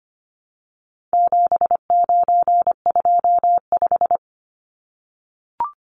Morse code, heard as a keyed tone, about 25 wpm, 700 Hz. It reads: 7925
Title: Numbers - 4-Digit 25wpm